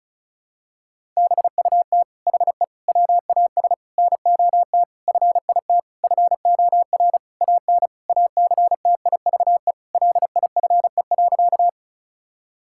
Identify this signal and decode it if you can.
Morse code, heard as a keyed tone, about 36 wpm, 700 Hz. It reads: BUT HE WAS NOT FIT FOR AN ACTIVE LIFE.
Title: Sentences from Top 1000 Words - Review Part 6 - 35wpm